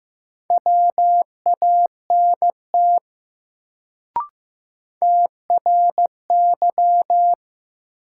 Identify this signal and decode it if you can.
Morse code, heard as a keyed tone, about 15 wpm, 700 Hz. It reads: WANT TRY
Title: Sets of 2 Words - New Word Drill - 15wpm